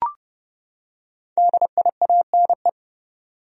Morse code, heard as a keyed tone, about 30 wpm, 700 Hz. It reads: DIANE
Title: USA Common Names - Max 5 Letters - 30wpm